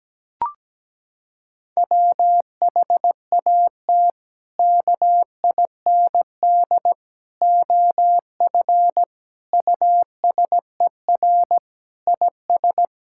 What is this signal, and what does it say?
Morse code, heard as a keyed tone, about 17 wpm, 700 Hz. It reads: WHAT KIND OF USER IS
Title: Sentences from Top 1000 Words - Review Part 5 - 17wpm